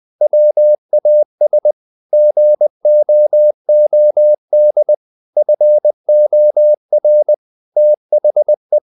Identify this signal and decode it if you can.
Morse code, heard as a keyed tone, about 20 wpm, 600 Hz. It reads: WAS GOOD FOR THE
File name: the old man and the sea / the old man and the sea part 1 20wpm